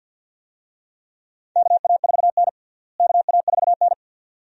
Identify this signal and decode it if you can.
Morse code, heard as a keyed tone, about 51 wpm, 700 Hz. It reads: KA4N KA4N
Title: USA Call Signs - 4 Characters - Repeated 2x - 50wpm